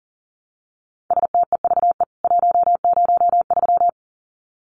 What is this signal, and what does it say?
Morse code, heard as a keyed tone, about 38 wpm, 700 Hz. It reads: STEVE 103
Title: CWT Exchange 40wpm